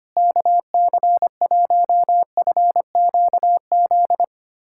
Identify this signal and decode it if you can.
Morse code, heard as a keyed tone, about 25 wpm, 700 Hz. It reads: KC1FQZ
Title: USA Callsign DE Callsign 25wpm